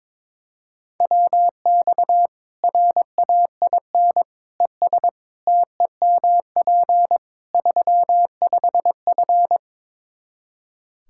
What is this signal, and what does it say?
Morse code, heard as a keyed tone, about 22 wpm, 700 Hz. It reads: WX RAIN ES TEMP 35F
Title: Practice QSO - Exchange 2 - Info - 22wpm